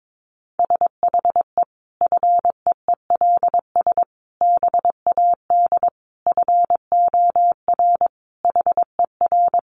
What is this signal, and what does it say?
Morse code, heard as a keyed tone, about 22 wpm, 700 Hz. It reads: SHE FEELS BAD FOR HER
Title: Additional Easy Phrases 22wpm